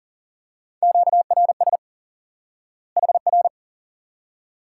Morse code, heard as a keyed tone, about 40 wpm, 700 Hz. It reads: QRS HR
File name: IQE Course 3 - Lesson 89 - YAGI - Two Elements - 40wpm